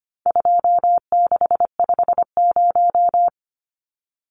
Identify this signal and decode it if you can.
Morse code, heard as a keyed tone, about 25 wpm, 700 Hz. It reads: 2650
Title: Numbers - 4-Digit 25wpm